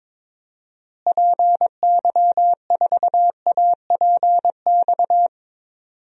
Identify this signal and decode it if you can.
Morse code, heard as a keyed tone, about 22 wpm, 700 Hz. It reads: PY4APX
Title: DXCC-100 Call Signs - 6 Characters - Repeated 3x - 22wpm